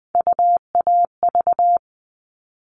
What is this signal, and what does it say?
Morse code, heard as a keyed tone, about 20 wpm, 700 Hz. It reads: UAV
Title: All Cut Numbers - 3-Digit 20wpm